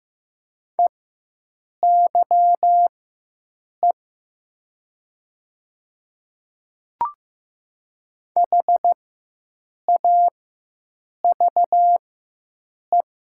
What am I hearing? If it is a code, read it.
Morse code, heard as a keyed tone, about 15 wpm, spaced as for 7 wpm, 700 Hz. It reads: EYE HAVE
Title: 4x Character Spacing - Top 300 Words - 15wpm